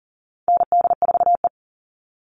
Morse code, heard as a keyed tone, about 40 wpm, 700 Hz. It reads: ND4E
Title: All Cut Numbers - 4-Digit 40wpm